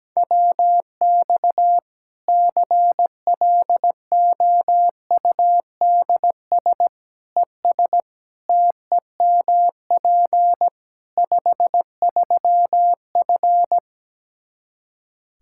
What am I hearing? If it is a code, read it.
Morse code, heard as a keyed tone, about 17 wpm, 700 Hz. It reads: WX CLOUDS ES TEMP 53F
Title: Practice QSO - Exchange 2 - Info - 17wpm